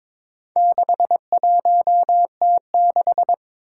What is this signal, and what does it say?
Morse code, heard as a keyed tone, about 21 wpm, 700 Hz. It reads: 61T6
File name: practice / 22wpm / Four Letter-Number Combinations - 22wpm